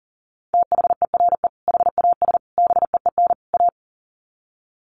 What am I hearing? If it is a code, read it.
Morse code, heard as a keyed tone, about 40 wpm, 700 Hz. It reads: THERE HAS BEEN A
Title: Sets of 4 Words 40wpm